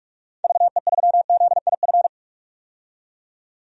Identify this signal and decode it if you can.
Morse code, heard as a keyed tone, about 46 wpm, 700 Hz. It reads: VE3ZIF
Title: DXCC-100 Call Signs - 6 Characters - 45wpm